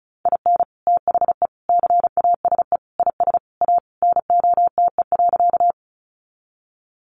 Morse code, heard as a keyed tone, about 34 wpm, 700 Hz. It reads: IN THE CASE IS A NOTE.
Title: Sentences from Top 700 Words 35wpm